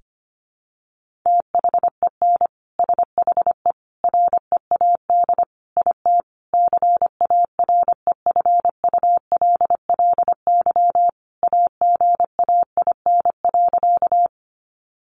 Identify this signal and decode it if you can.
Morse code, heard as a keyed tone, about 25 wpm, 700 Hz. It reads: THEN SHE READ IT CAREFULLY AGAIN.
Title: Sentences from Top 1300 Words 25wpm